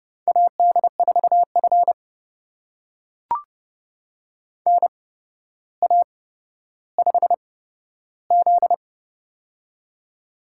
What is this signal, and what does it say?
Morse code, heard as a keyed tone, about 29 wpm, spaced as for 13 wpm, 700 Hz. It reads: AD4F NA5Z
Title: USA Call Signs - 4 Characters - 8x Character Spacing - 30wpm